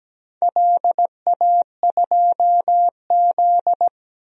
Morse code, heard as a keyed tone, about 17 wpm, 700 Hz. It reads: LA2Z
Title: DXCC-100 Call Signs - 4 Characters - 17wpm